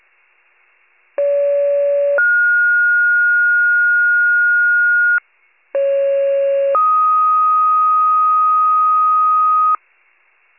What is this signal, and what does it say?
Two-tone page, heard as a one-second tone, then a three-second tone, about 570 then 1430 Hz, about 550 then 1250 Hz. It is heard from 1.2 to 9.8 s.